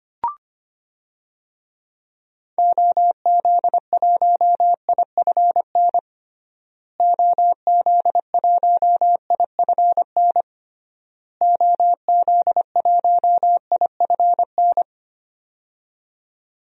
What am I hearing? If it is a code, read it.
Morse code, heard as a keyed tone, about 25 wpm, 700 Hz. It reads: OZ1IFN OZ1IFN OZ1IFN
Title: DXCC-100 Call Signs - 6 Characters - Repeated 3x - 25wpm